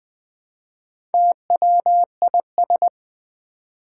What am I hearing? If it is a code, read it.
Morse code, heard as a keyed tone, about 20 wpm, 700 Hz. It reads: TWIS
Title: Build A Word - 1 Letter At A Time - 3 to 5-Letter Words - 20wpm